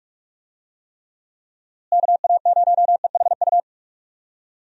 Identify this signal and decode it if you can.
Morse code, heard as a keyed tone, about 42 wpm, 700 Hz. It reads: KA0EHU
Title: USA Call Signs - 6 Characters - Repeated 3x - 45wpm